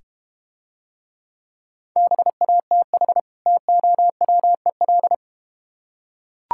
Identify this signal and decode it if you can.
Morse code, heard as a keyed tone, about 30 wpm, 700 Hz. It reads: BATH TOWEL
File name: practice / 25wpm / Difficult Words - Speed-Racing with Hints - 25wpm